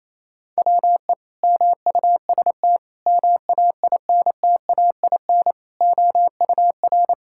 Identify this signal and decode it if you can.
Morse code, heard as a keyed tone, about 28 wpm, 700 Hz. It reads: WE MUST MAINTAIN OUR
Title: Sentences from Top 1000 Words - Review Part 5 - 28wpm